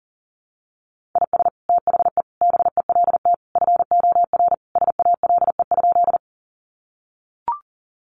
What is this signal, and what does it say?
Morse code, heard as a keyed tone, about 40 wpm, 700 Hz. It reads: IS THE BELT FOR SALE?
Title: Sentences from Top 1400 Words 40wpm